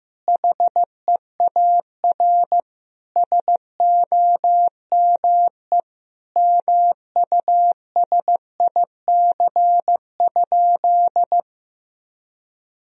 Morse code, heard as a keyed tone, about 15 wpm, 700 Hz. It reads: HEAR SOME MUSIC?
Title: Sentences from Top 1000 Words - Review Part 1 - 15wpm